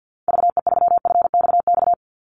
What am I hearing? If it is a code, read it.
Morse code, heard as a keyed tone, about 49 wpm, 700 Hz. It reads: VE3FXX
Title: Call Signs Repeated 3x - 50wpm